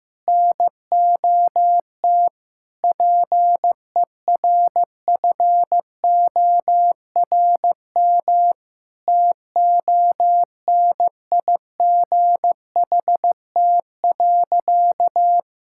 Morse code, heard as a keyed tone, about 16 wpm, 700 Hz. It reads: NOT PERFORM TONIGHT.
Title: Sentences from Top 1000 Words - Review Part 5 - 15wpm